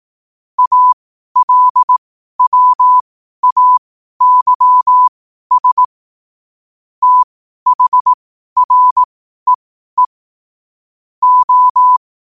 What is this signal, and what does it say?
Morse code, heard as a keyed tone, about 18 wpm, spaced as for 12 wpm, 1000 Hz. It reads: ALWAYS THREE O
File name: qotdfiles / 12wpm